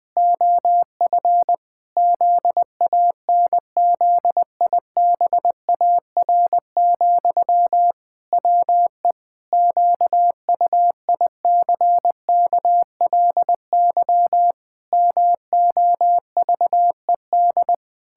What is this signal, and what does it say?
Morse code, heard as a keyed tone, about 20 wpm, 700 Hz. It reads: OF ZANZIBAR, WE QUICKLY MOVED
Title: Pangram Challenge v2 - 20wpm